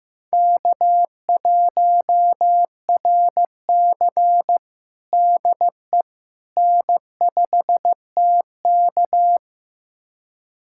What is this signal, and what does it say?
Morse code, heard as a keyed tone, about 15 wpm, 700 Hz. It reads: K1RC DE N5TK
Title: USA Callsign DE Callsign - 4 Characters - 15wpm